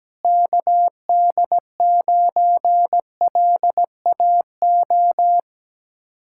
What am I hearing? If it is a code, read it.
Morse code, heard as a keyed tone, about 17 wpm, 700 Hz. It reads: KD9LAO
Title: Speed-Racing Common QSO Phrases - 17wpm